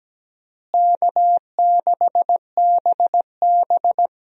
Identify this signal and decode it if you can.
Morse code, heard as a keyed tone, about 17 wpm, 700 Hz. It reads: K6BB
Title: USA Call Signs - 4 Characters - Repeated 3x - 17wpm